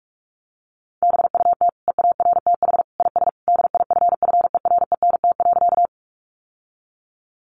Morse code, heard as a keyed tone, about 44 wpm, 700 Hz. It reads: BUT EARTH IS DIFFERENT.
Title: Sentences from Top 600 Words 45wpm